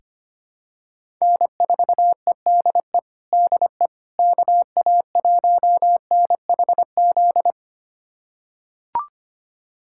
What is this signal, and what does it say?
Morse code, heard as a keyed tone, about 25 wpm, 700 Hz. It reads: N4EDE DE KA1NHZ
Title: USA Callsign DE Callsign 25wpm